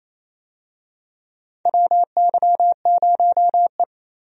Morse code, heard as a keyed tone, about 28 wpm, 700 Hz. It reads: WY0E
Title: USA Call Signs - 4 Characters - Repeated 3x - 28wpm